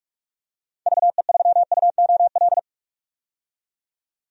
Morse code, heard as a keyed tone, about 42 wpm, 690 Hz. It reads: VE3UOL
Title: DXCC-100 Call Signs - 6 Characters - 45wpm